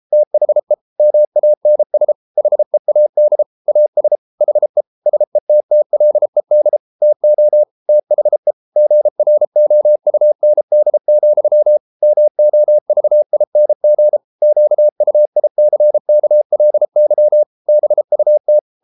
Morse code, heard as a keyed tone, about 33 wpm, 600 Hz. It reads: THE MANS HEAD AS HE SETTLED TO THE GROUND, MOVING QUICKLY BUT